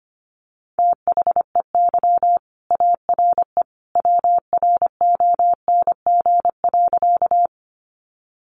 Morse code, heard as a keyed tone, about 25 wpm, 700 Hz. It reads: THEY ARE WRONG.
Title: Sentences from Top 900 Words 25wpm